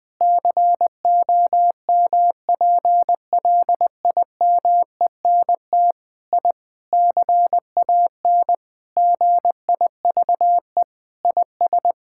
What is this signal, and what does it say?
Morse code, heard as a keyed tone, about 20 wpm, 700 Hz. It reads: COMPLIMENT I CAN GIVE IS
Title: Sentences with 8-Word Phrases - Spoken Beginning - 20wpm